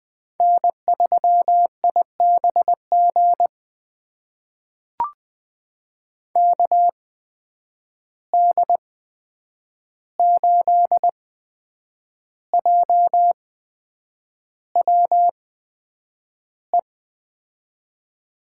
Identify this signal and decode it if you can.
Morse code, heard as a keyed tone, about 20 wpm, spaced as for 8 wpm, 700 Hz. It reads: N3IBG KD8JWE
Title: USA Call Signs - 8x Character Spacing - 20wpm